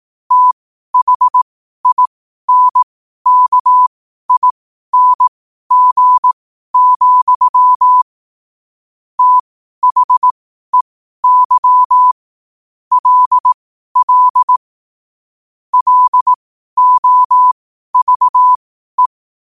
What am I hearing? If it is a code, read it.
Morse code, heard as a keyed tone, about 18 wpm, spaced as for 12 wpm, 1000 Hz. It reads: THINKING, THEY LL LOVE
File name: qotdfiles / 12wpm